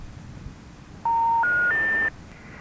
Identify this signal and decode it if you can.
Special information tone.